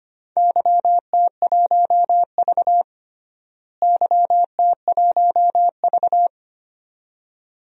Morse code, heard as a keyed tone, about 25 wpm, 700 Hz. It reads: YT1V YT1V
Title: DXCC-100 Call Signs - 4 Characters - Repeated 2x - 25wpm